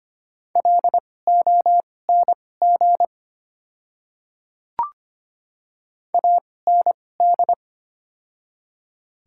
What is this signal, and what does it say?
Morse code, heard as a keyed tone, about 25 wpm, spaced as for 18 wpm, 700 Hz. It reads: LONG AND
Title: 2x Character Spacing - Top 100 Words - 25wpm